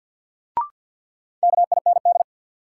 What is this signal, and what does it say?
Morse code, heard as a keyed tone, about 53 wpm, 700 Hz. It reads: KIND
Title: Sentences from Top 200 Words - New Word Drill - 50wpm